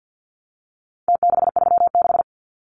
Morse code, heard as a keyed tone, about 50 wpm, 700 Hz. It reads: T636
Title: Numbers - Contest Style - 4-Digit 50wpm